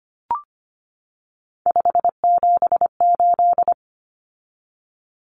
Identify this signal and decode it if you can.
Morse code, heard as a keyed tone, about 25 wpm, 700 Hz. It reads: 578
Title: Numbers - 3-Digit 25wpm